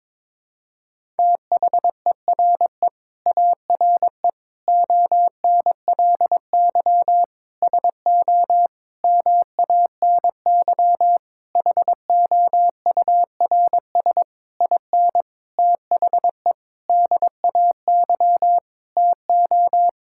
Morse code, heard as a keyed tone, about 22 wpm, 700 Hz. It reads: THERE ARE ONLY SO MANY HOURS IN THE DAY TO
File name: Sentences with 9-Word Phrases - 22wpm